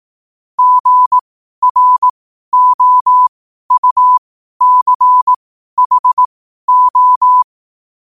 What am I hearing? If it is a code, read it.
Morse code, heard as a keyed tone, about 18 wpm, spaced as for 12 wpm, 1000 Hz. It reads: GROUCHO
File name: 12wpm